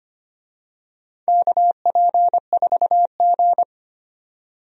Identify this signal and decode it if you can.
Morse code, heard as a keyed tone, about 25 wpm, 700 Hz. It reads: KP4G
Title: DXCC-100 Call Signs - 4 Characters - 25wpm